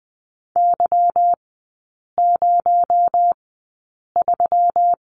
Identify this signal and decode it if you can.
Morse code, heard as a keyed tone, about 20 wpm, spaced as for 8 wpm, 700 Hz. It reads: Y03
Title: 218 Lesson 35e - Number 0 - Three Characters 20wpm